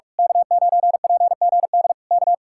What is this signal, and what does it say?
Morse code, heard as a keyed tone, about 46 wpm, 700 Hz. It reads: K9PGD K
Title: Speed-Racing Common QSO Phrases - 45wpm